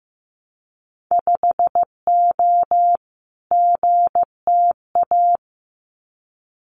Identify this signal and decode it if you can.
Morse code, heard as a keyed tone, about 15 wpm, 700 Hz. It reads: 5O GTA
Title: Winter Field Day Exchanges Encore 15wpm